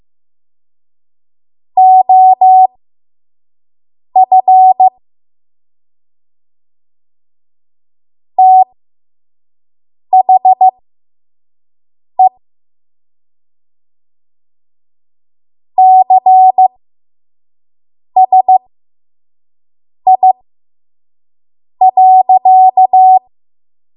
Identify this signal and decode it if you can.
Morse code, heard as a keyed tone, about 15 wpm, spaced as for 5 wpm, 750 Hz. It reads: OF THE CSI.
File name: Morse / 05WPM